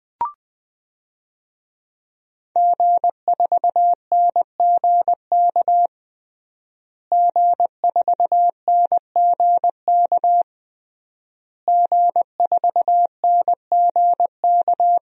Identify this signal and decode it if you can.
Morse code, heard as a keyed tone, about 20 wpm, 700 Hz. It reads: G4NGK G4NGK G4NGK
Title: DXCC-100 Call Signs - 5 Characters - Repeated 3x - 20wpm